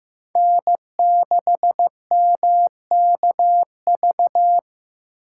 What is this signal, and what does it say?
Morse code, heard as a keyed tone, about 18 wpm, 700 Hz. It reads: N6MKV
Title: Speed Racing - USA 1x3 Call Signs 15wpm